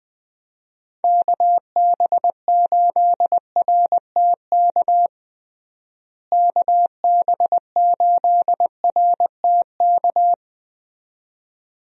Morse code, heard as a keyed tone, about 20 wpm, 700 Hz. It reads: KB8RTK KB8RTK
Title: USA Call Signs - 6 Characters - Repeated 2x - 20wpm